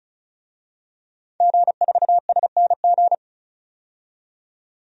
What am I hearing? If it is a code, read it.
Morse code, heard as a keyed tone, about 36 wpm, 700 Hz. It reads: G4SNG
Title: DXCC-100 Call Signs - 5 Characters - 35wpm